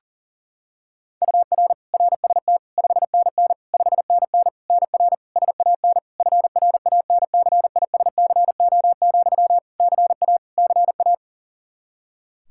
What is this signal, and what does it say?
Morse code, heard as a keyed tone, about 37 wpm, 700 Hz. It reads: UR RST 5NN 5NN NR SAN FRANCISCO, CA CA
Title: Practice QSO - Exchange 1 - Beginning - 40wpm